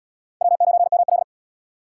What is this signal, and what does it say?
Morse code, heard as a keyed tone, about 68 wpm, 700 Hz. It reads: W9ML